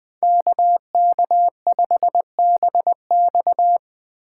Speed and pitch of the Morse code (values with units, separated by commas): 20 wpm, 700 Hz